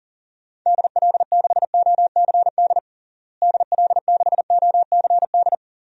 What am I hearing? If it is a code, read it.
Morse code, heard as a keyed tone, about 40 wpm, 700 Hz. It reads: DL6OCD DL6OCD
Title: DXCC-100 Call Signs - 6 Characters - Repeated 3x - 40wpm